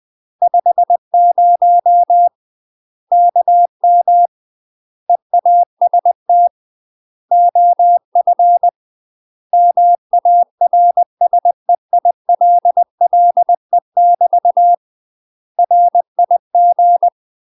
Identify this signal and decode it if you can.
Morse code, heard as a keyed tone, about 20 wpm, 700 Hz. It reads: 50 KM EAST OF MARSEILLE <BT> RIG
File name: UK MASTER QSOs TRAINING / 20WPM / UK MASTER QSOs 0053